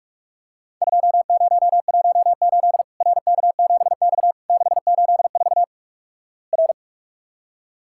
Morse code, heard as a keyed tone, about 45 wpm, 700 Hz. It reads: 2018 RK7X 684 R
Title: Worked All Europe DX Contest - QTC Exchange - 45wpm